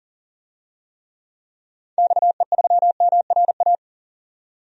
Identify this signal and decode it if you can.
Morse code, heard as a keyed tone, about 38 wpm, 700 Hz. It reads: XE3MRA
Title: DXCC-100 Call Signs - 6 Characters - 40wpm